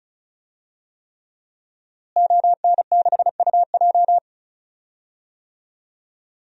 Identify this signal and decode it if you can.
Morse code, heard as a keyed tone, about 34 wpm, 700 Hz. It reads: ON6UJ
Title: DXCC-100 Call Signs - 5 Characters - Repeated 3x - 35wpm